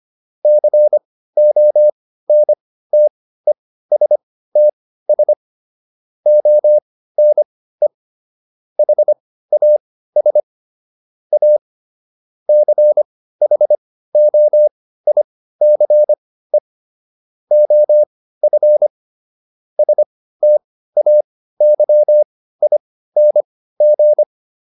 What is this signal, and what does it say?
Morse code, heard as a keyed tone, about 25 wpm, spaced as for 15 wpm, 600 Hz. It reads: CONTESTS ONE HAS A CHOICE OF STAYING